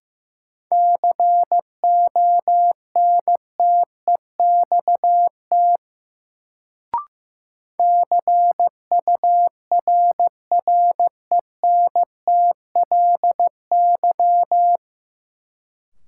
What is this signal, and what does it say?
Morse code, heard as a keyed tone, about 15 wpm, 700 Hz. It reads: CONTEXT CURRENTLY
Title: Sentences from Top 1000 Words - Review Part 7 - New Word Drill - 15wpm